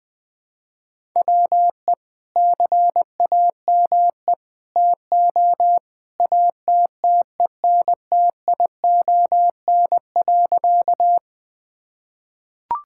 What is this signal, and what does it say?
Morse code, heard as a keyed tone, about 20 wpm, 700 Hz. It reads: WE CAME TO ATTENTION.
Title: Sentences from Top 1000 Words - Review Part 6 - 20wpm